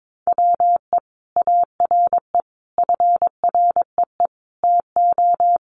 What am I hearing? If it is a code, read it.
Morse code, heard as a keyed tone, about 22 wpm, 700 Hz. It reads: WE ARE FREE TO
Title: Sentences from Top 500 Words 22wpm